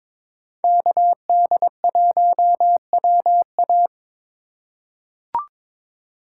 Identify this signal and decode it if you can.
Morse code, heard as a keyed tone, about 22 wpm, 700 Hz. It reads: KD1WA
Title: USA Call Signs - 5 Characters - 4x Character Spacing - 22wpm